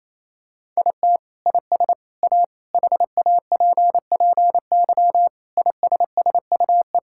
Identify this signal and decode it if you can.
Morse code, heard as a keyed tone, about 28 wpm, 700 Hz. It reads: IT IS A HAPPY ISSUE
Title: Sentences from Top 1000 Words - Review Part 6 - 28wpm